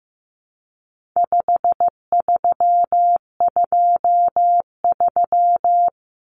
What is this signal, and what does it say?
Morse code, heard as a keyed tone, about 15 wpm, 700 Hz. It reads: 5323
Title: Numbers - 4-Digit 15wpm